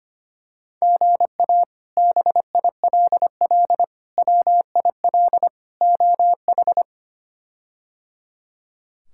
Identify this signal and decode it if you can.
Morse code, heard as a keyed tone, about 25 wpm, 700 Hz. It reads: GA BILL WIL OH
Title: K1USN SST QSO Practice - 25wpm